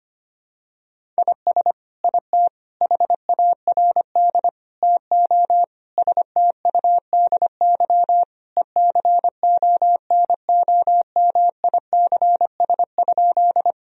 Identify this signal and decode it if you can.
Morse code, heard as a keyed tone, about 25 wpm, 700 Hz. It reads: IS IT HARD TO STUDY ECONOMICS?
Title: Sentences from Top 1000 Words - Review Part 3 - 25wpm